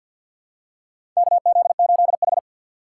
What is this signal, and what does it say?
Morse code, heard as a keyed tone, about 53 wpm, 700 Hz. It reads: KZ8H
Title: USA Call Signs - 4x Character Spacing - 50wpm